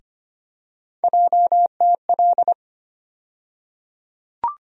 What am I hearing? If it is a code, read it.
Morse code, heard as a keyed tone, about 25 wpm, 700 Hz. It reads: JTL
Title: USA Call Sign Postfixes - 3 Letters 25wpm